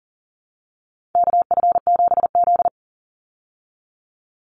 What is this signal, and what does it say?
Morse code, heard as a keyed tone, about 40 wpm, 700 Hz. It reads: KF7Z
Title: Call Signs Repeated 3x - 40wpm